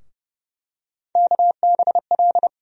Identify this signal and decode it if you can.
Morse code, heard as a keyed tone, about 30 wpm, 700 Hz. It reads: KBL
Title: USA Call Sign Postfixes - 3 Letters 30wpm